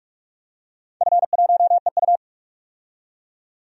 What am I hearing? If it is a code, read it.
Morse code, heard as a keyed tone, about 46 wpm, 700 Hz. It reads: F1EU
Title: DXCC-100 Call Signs - 4 Characters - 45wpm